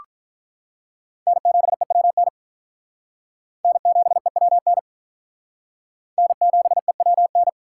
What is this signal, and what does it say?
Morse code, heard as a keyed tone, about 44 wpm, 700 Hz. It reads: N7EWN N7EWN N7EWN